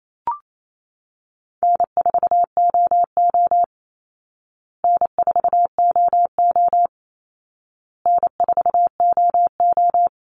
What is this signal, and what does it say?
Morse code, heard as a keyed tone, about 28 wpm, 700 Hz. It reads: N4OO N4OO N4OO
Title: Call Signs Repeated 2x - 28wpm